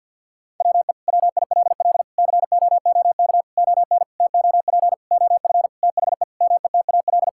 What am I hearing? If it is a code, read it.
Morse code, heard as a keyed tone, about 50 wpm, 700 Hz. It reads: WE WILL COOK ON TOP OF THE METAL
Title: Sentences from Top 1000 Words - Review Part 6 - 50wpm